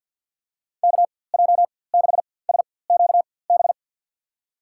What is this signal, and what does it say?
Morse code, heard as a keyed tone, about 49 wpm, spaced as for 23 wpm, 700 Hz. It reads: KJ6SQB
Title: USA Call Signs - 6 Characters - 4x Character Spacing - 50wpm